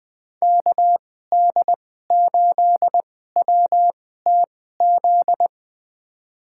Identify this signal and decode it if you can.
Morse code, heard as a keyed tone, about 20 wpm, spaced as for 14 wpm, 700 Hz. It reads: KD8WTZ
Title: USA Call Signs - 2x Character Spacing - 20wpm